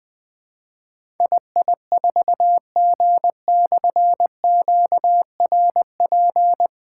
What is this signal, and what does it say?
Morse code, heard as a keyed tone, about 20 wpm, 700 Hz. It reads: II4G/QRP
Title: Difficult Call Signs 20wpm